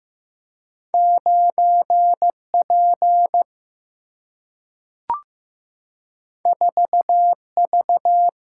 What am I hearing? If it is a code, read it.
Morse code, heard as a keyed tone, about 16 wpm, 700 Hz. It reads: 9P 4V
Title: Single-Letter Single-Number 15wpm